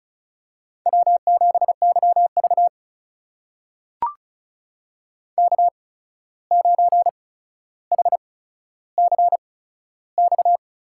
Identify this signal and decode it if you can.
Morse code, heard as a keyed tone, about 36 wpm, spaced as for 14 wpm, 700 Hz. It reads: W7YV K9HCX
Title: USA Call Signs - 8x Character Spacing - 35wpm